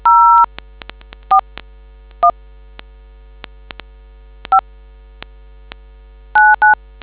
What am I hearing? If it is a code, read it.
DTMF, heard as telephone touch tones, tones of about 130 ms. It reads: 0 4 1 5 99